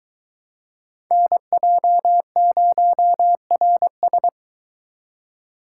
Morse code, heard as a keyed tone, about 22 wpm, 700 Hz. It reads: NJ0RS